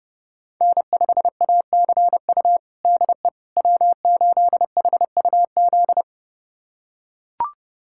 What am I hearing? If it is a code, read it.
Morse code, heard as a keyed tone, about 32 wpm, 700 Hz. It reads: N5ACU DE W8HUZ
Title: USA Callsign DE Callsign - 5 Characters - 30wpm